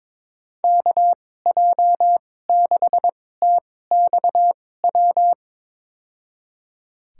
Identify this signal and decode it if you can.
Morse code, heard as a keyed tone, about 22 wpm, spaced as for 16 wpm, 700 Hz. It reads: KJ6TXW
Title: USA Call Signs - 6 Characters - 2x Character Spacing - 22wpm